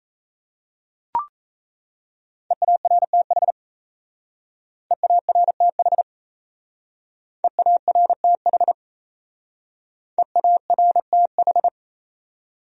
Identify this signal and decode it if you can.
Morse code, heard as a keyed tone, about 33 wpm, 700 Hz. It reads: EARTH EARTH EARTH EARTH
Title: Speed-Racing Top 200 Words 28wpm